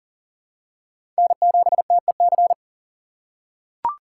Morse code, heard as a keyed tone, about 40 wpm, 700 Hz. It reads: N7TEC
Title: USA Call Signs - 5 Characters - 6x Character Spacing - 40wpm